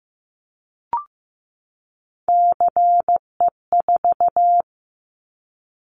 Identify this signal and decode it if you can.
Morse code, heard as a keyed tone, about 15 wpm, 700 Hz. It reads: CE4
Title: Callsign Prefixes for Top 100 DXCC - 15wpm